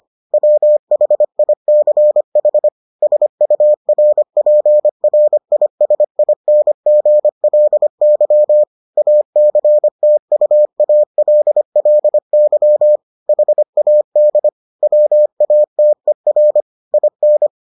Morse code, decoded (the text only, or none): WHICH SURPRISINGLY ACTUALLY HAD WATER IN